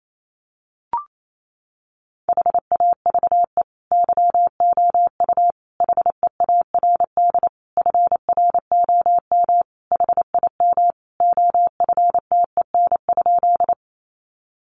Morse code, heard as a keyed tone, about 28 wpm, 700 Hz. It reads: HAVE YOU HEARD FROM HIM OFTEN?
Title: Sentences from Top 400 Words 28wpm